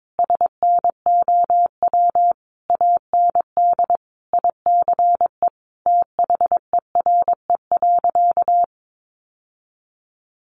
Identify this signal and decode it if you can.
Morse code, heard as a keyed tone, about 22 wpm, 700 Hz. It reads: SNOW AND ICE THERE.
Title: Sentences from Top 1000 Words 22wpm